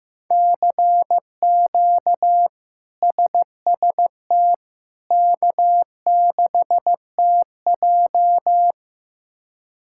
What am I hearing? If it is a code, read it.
Morse code, heard as a keyed tone, about 15 wpm, 700 Hz. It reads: CQ SST K6TJ
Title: K1USN SST QSO Practice - 15wpm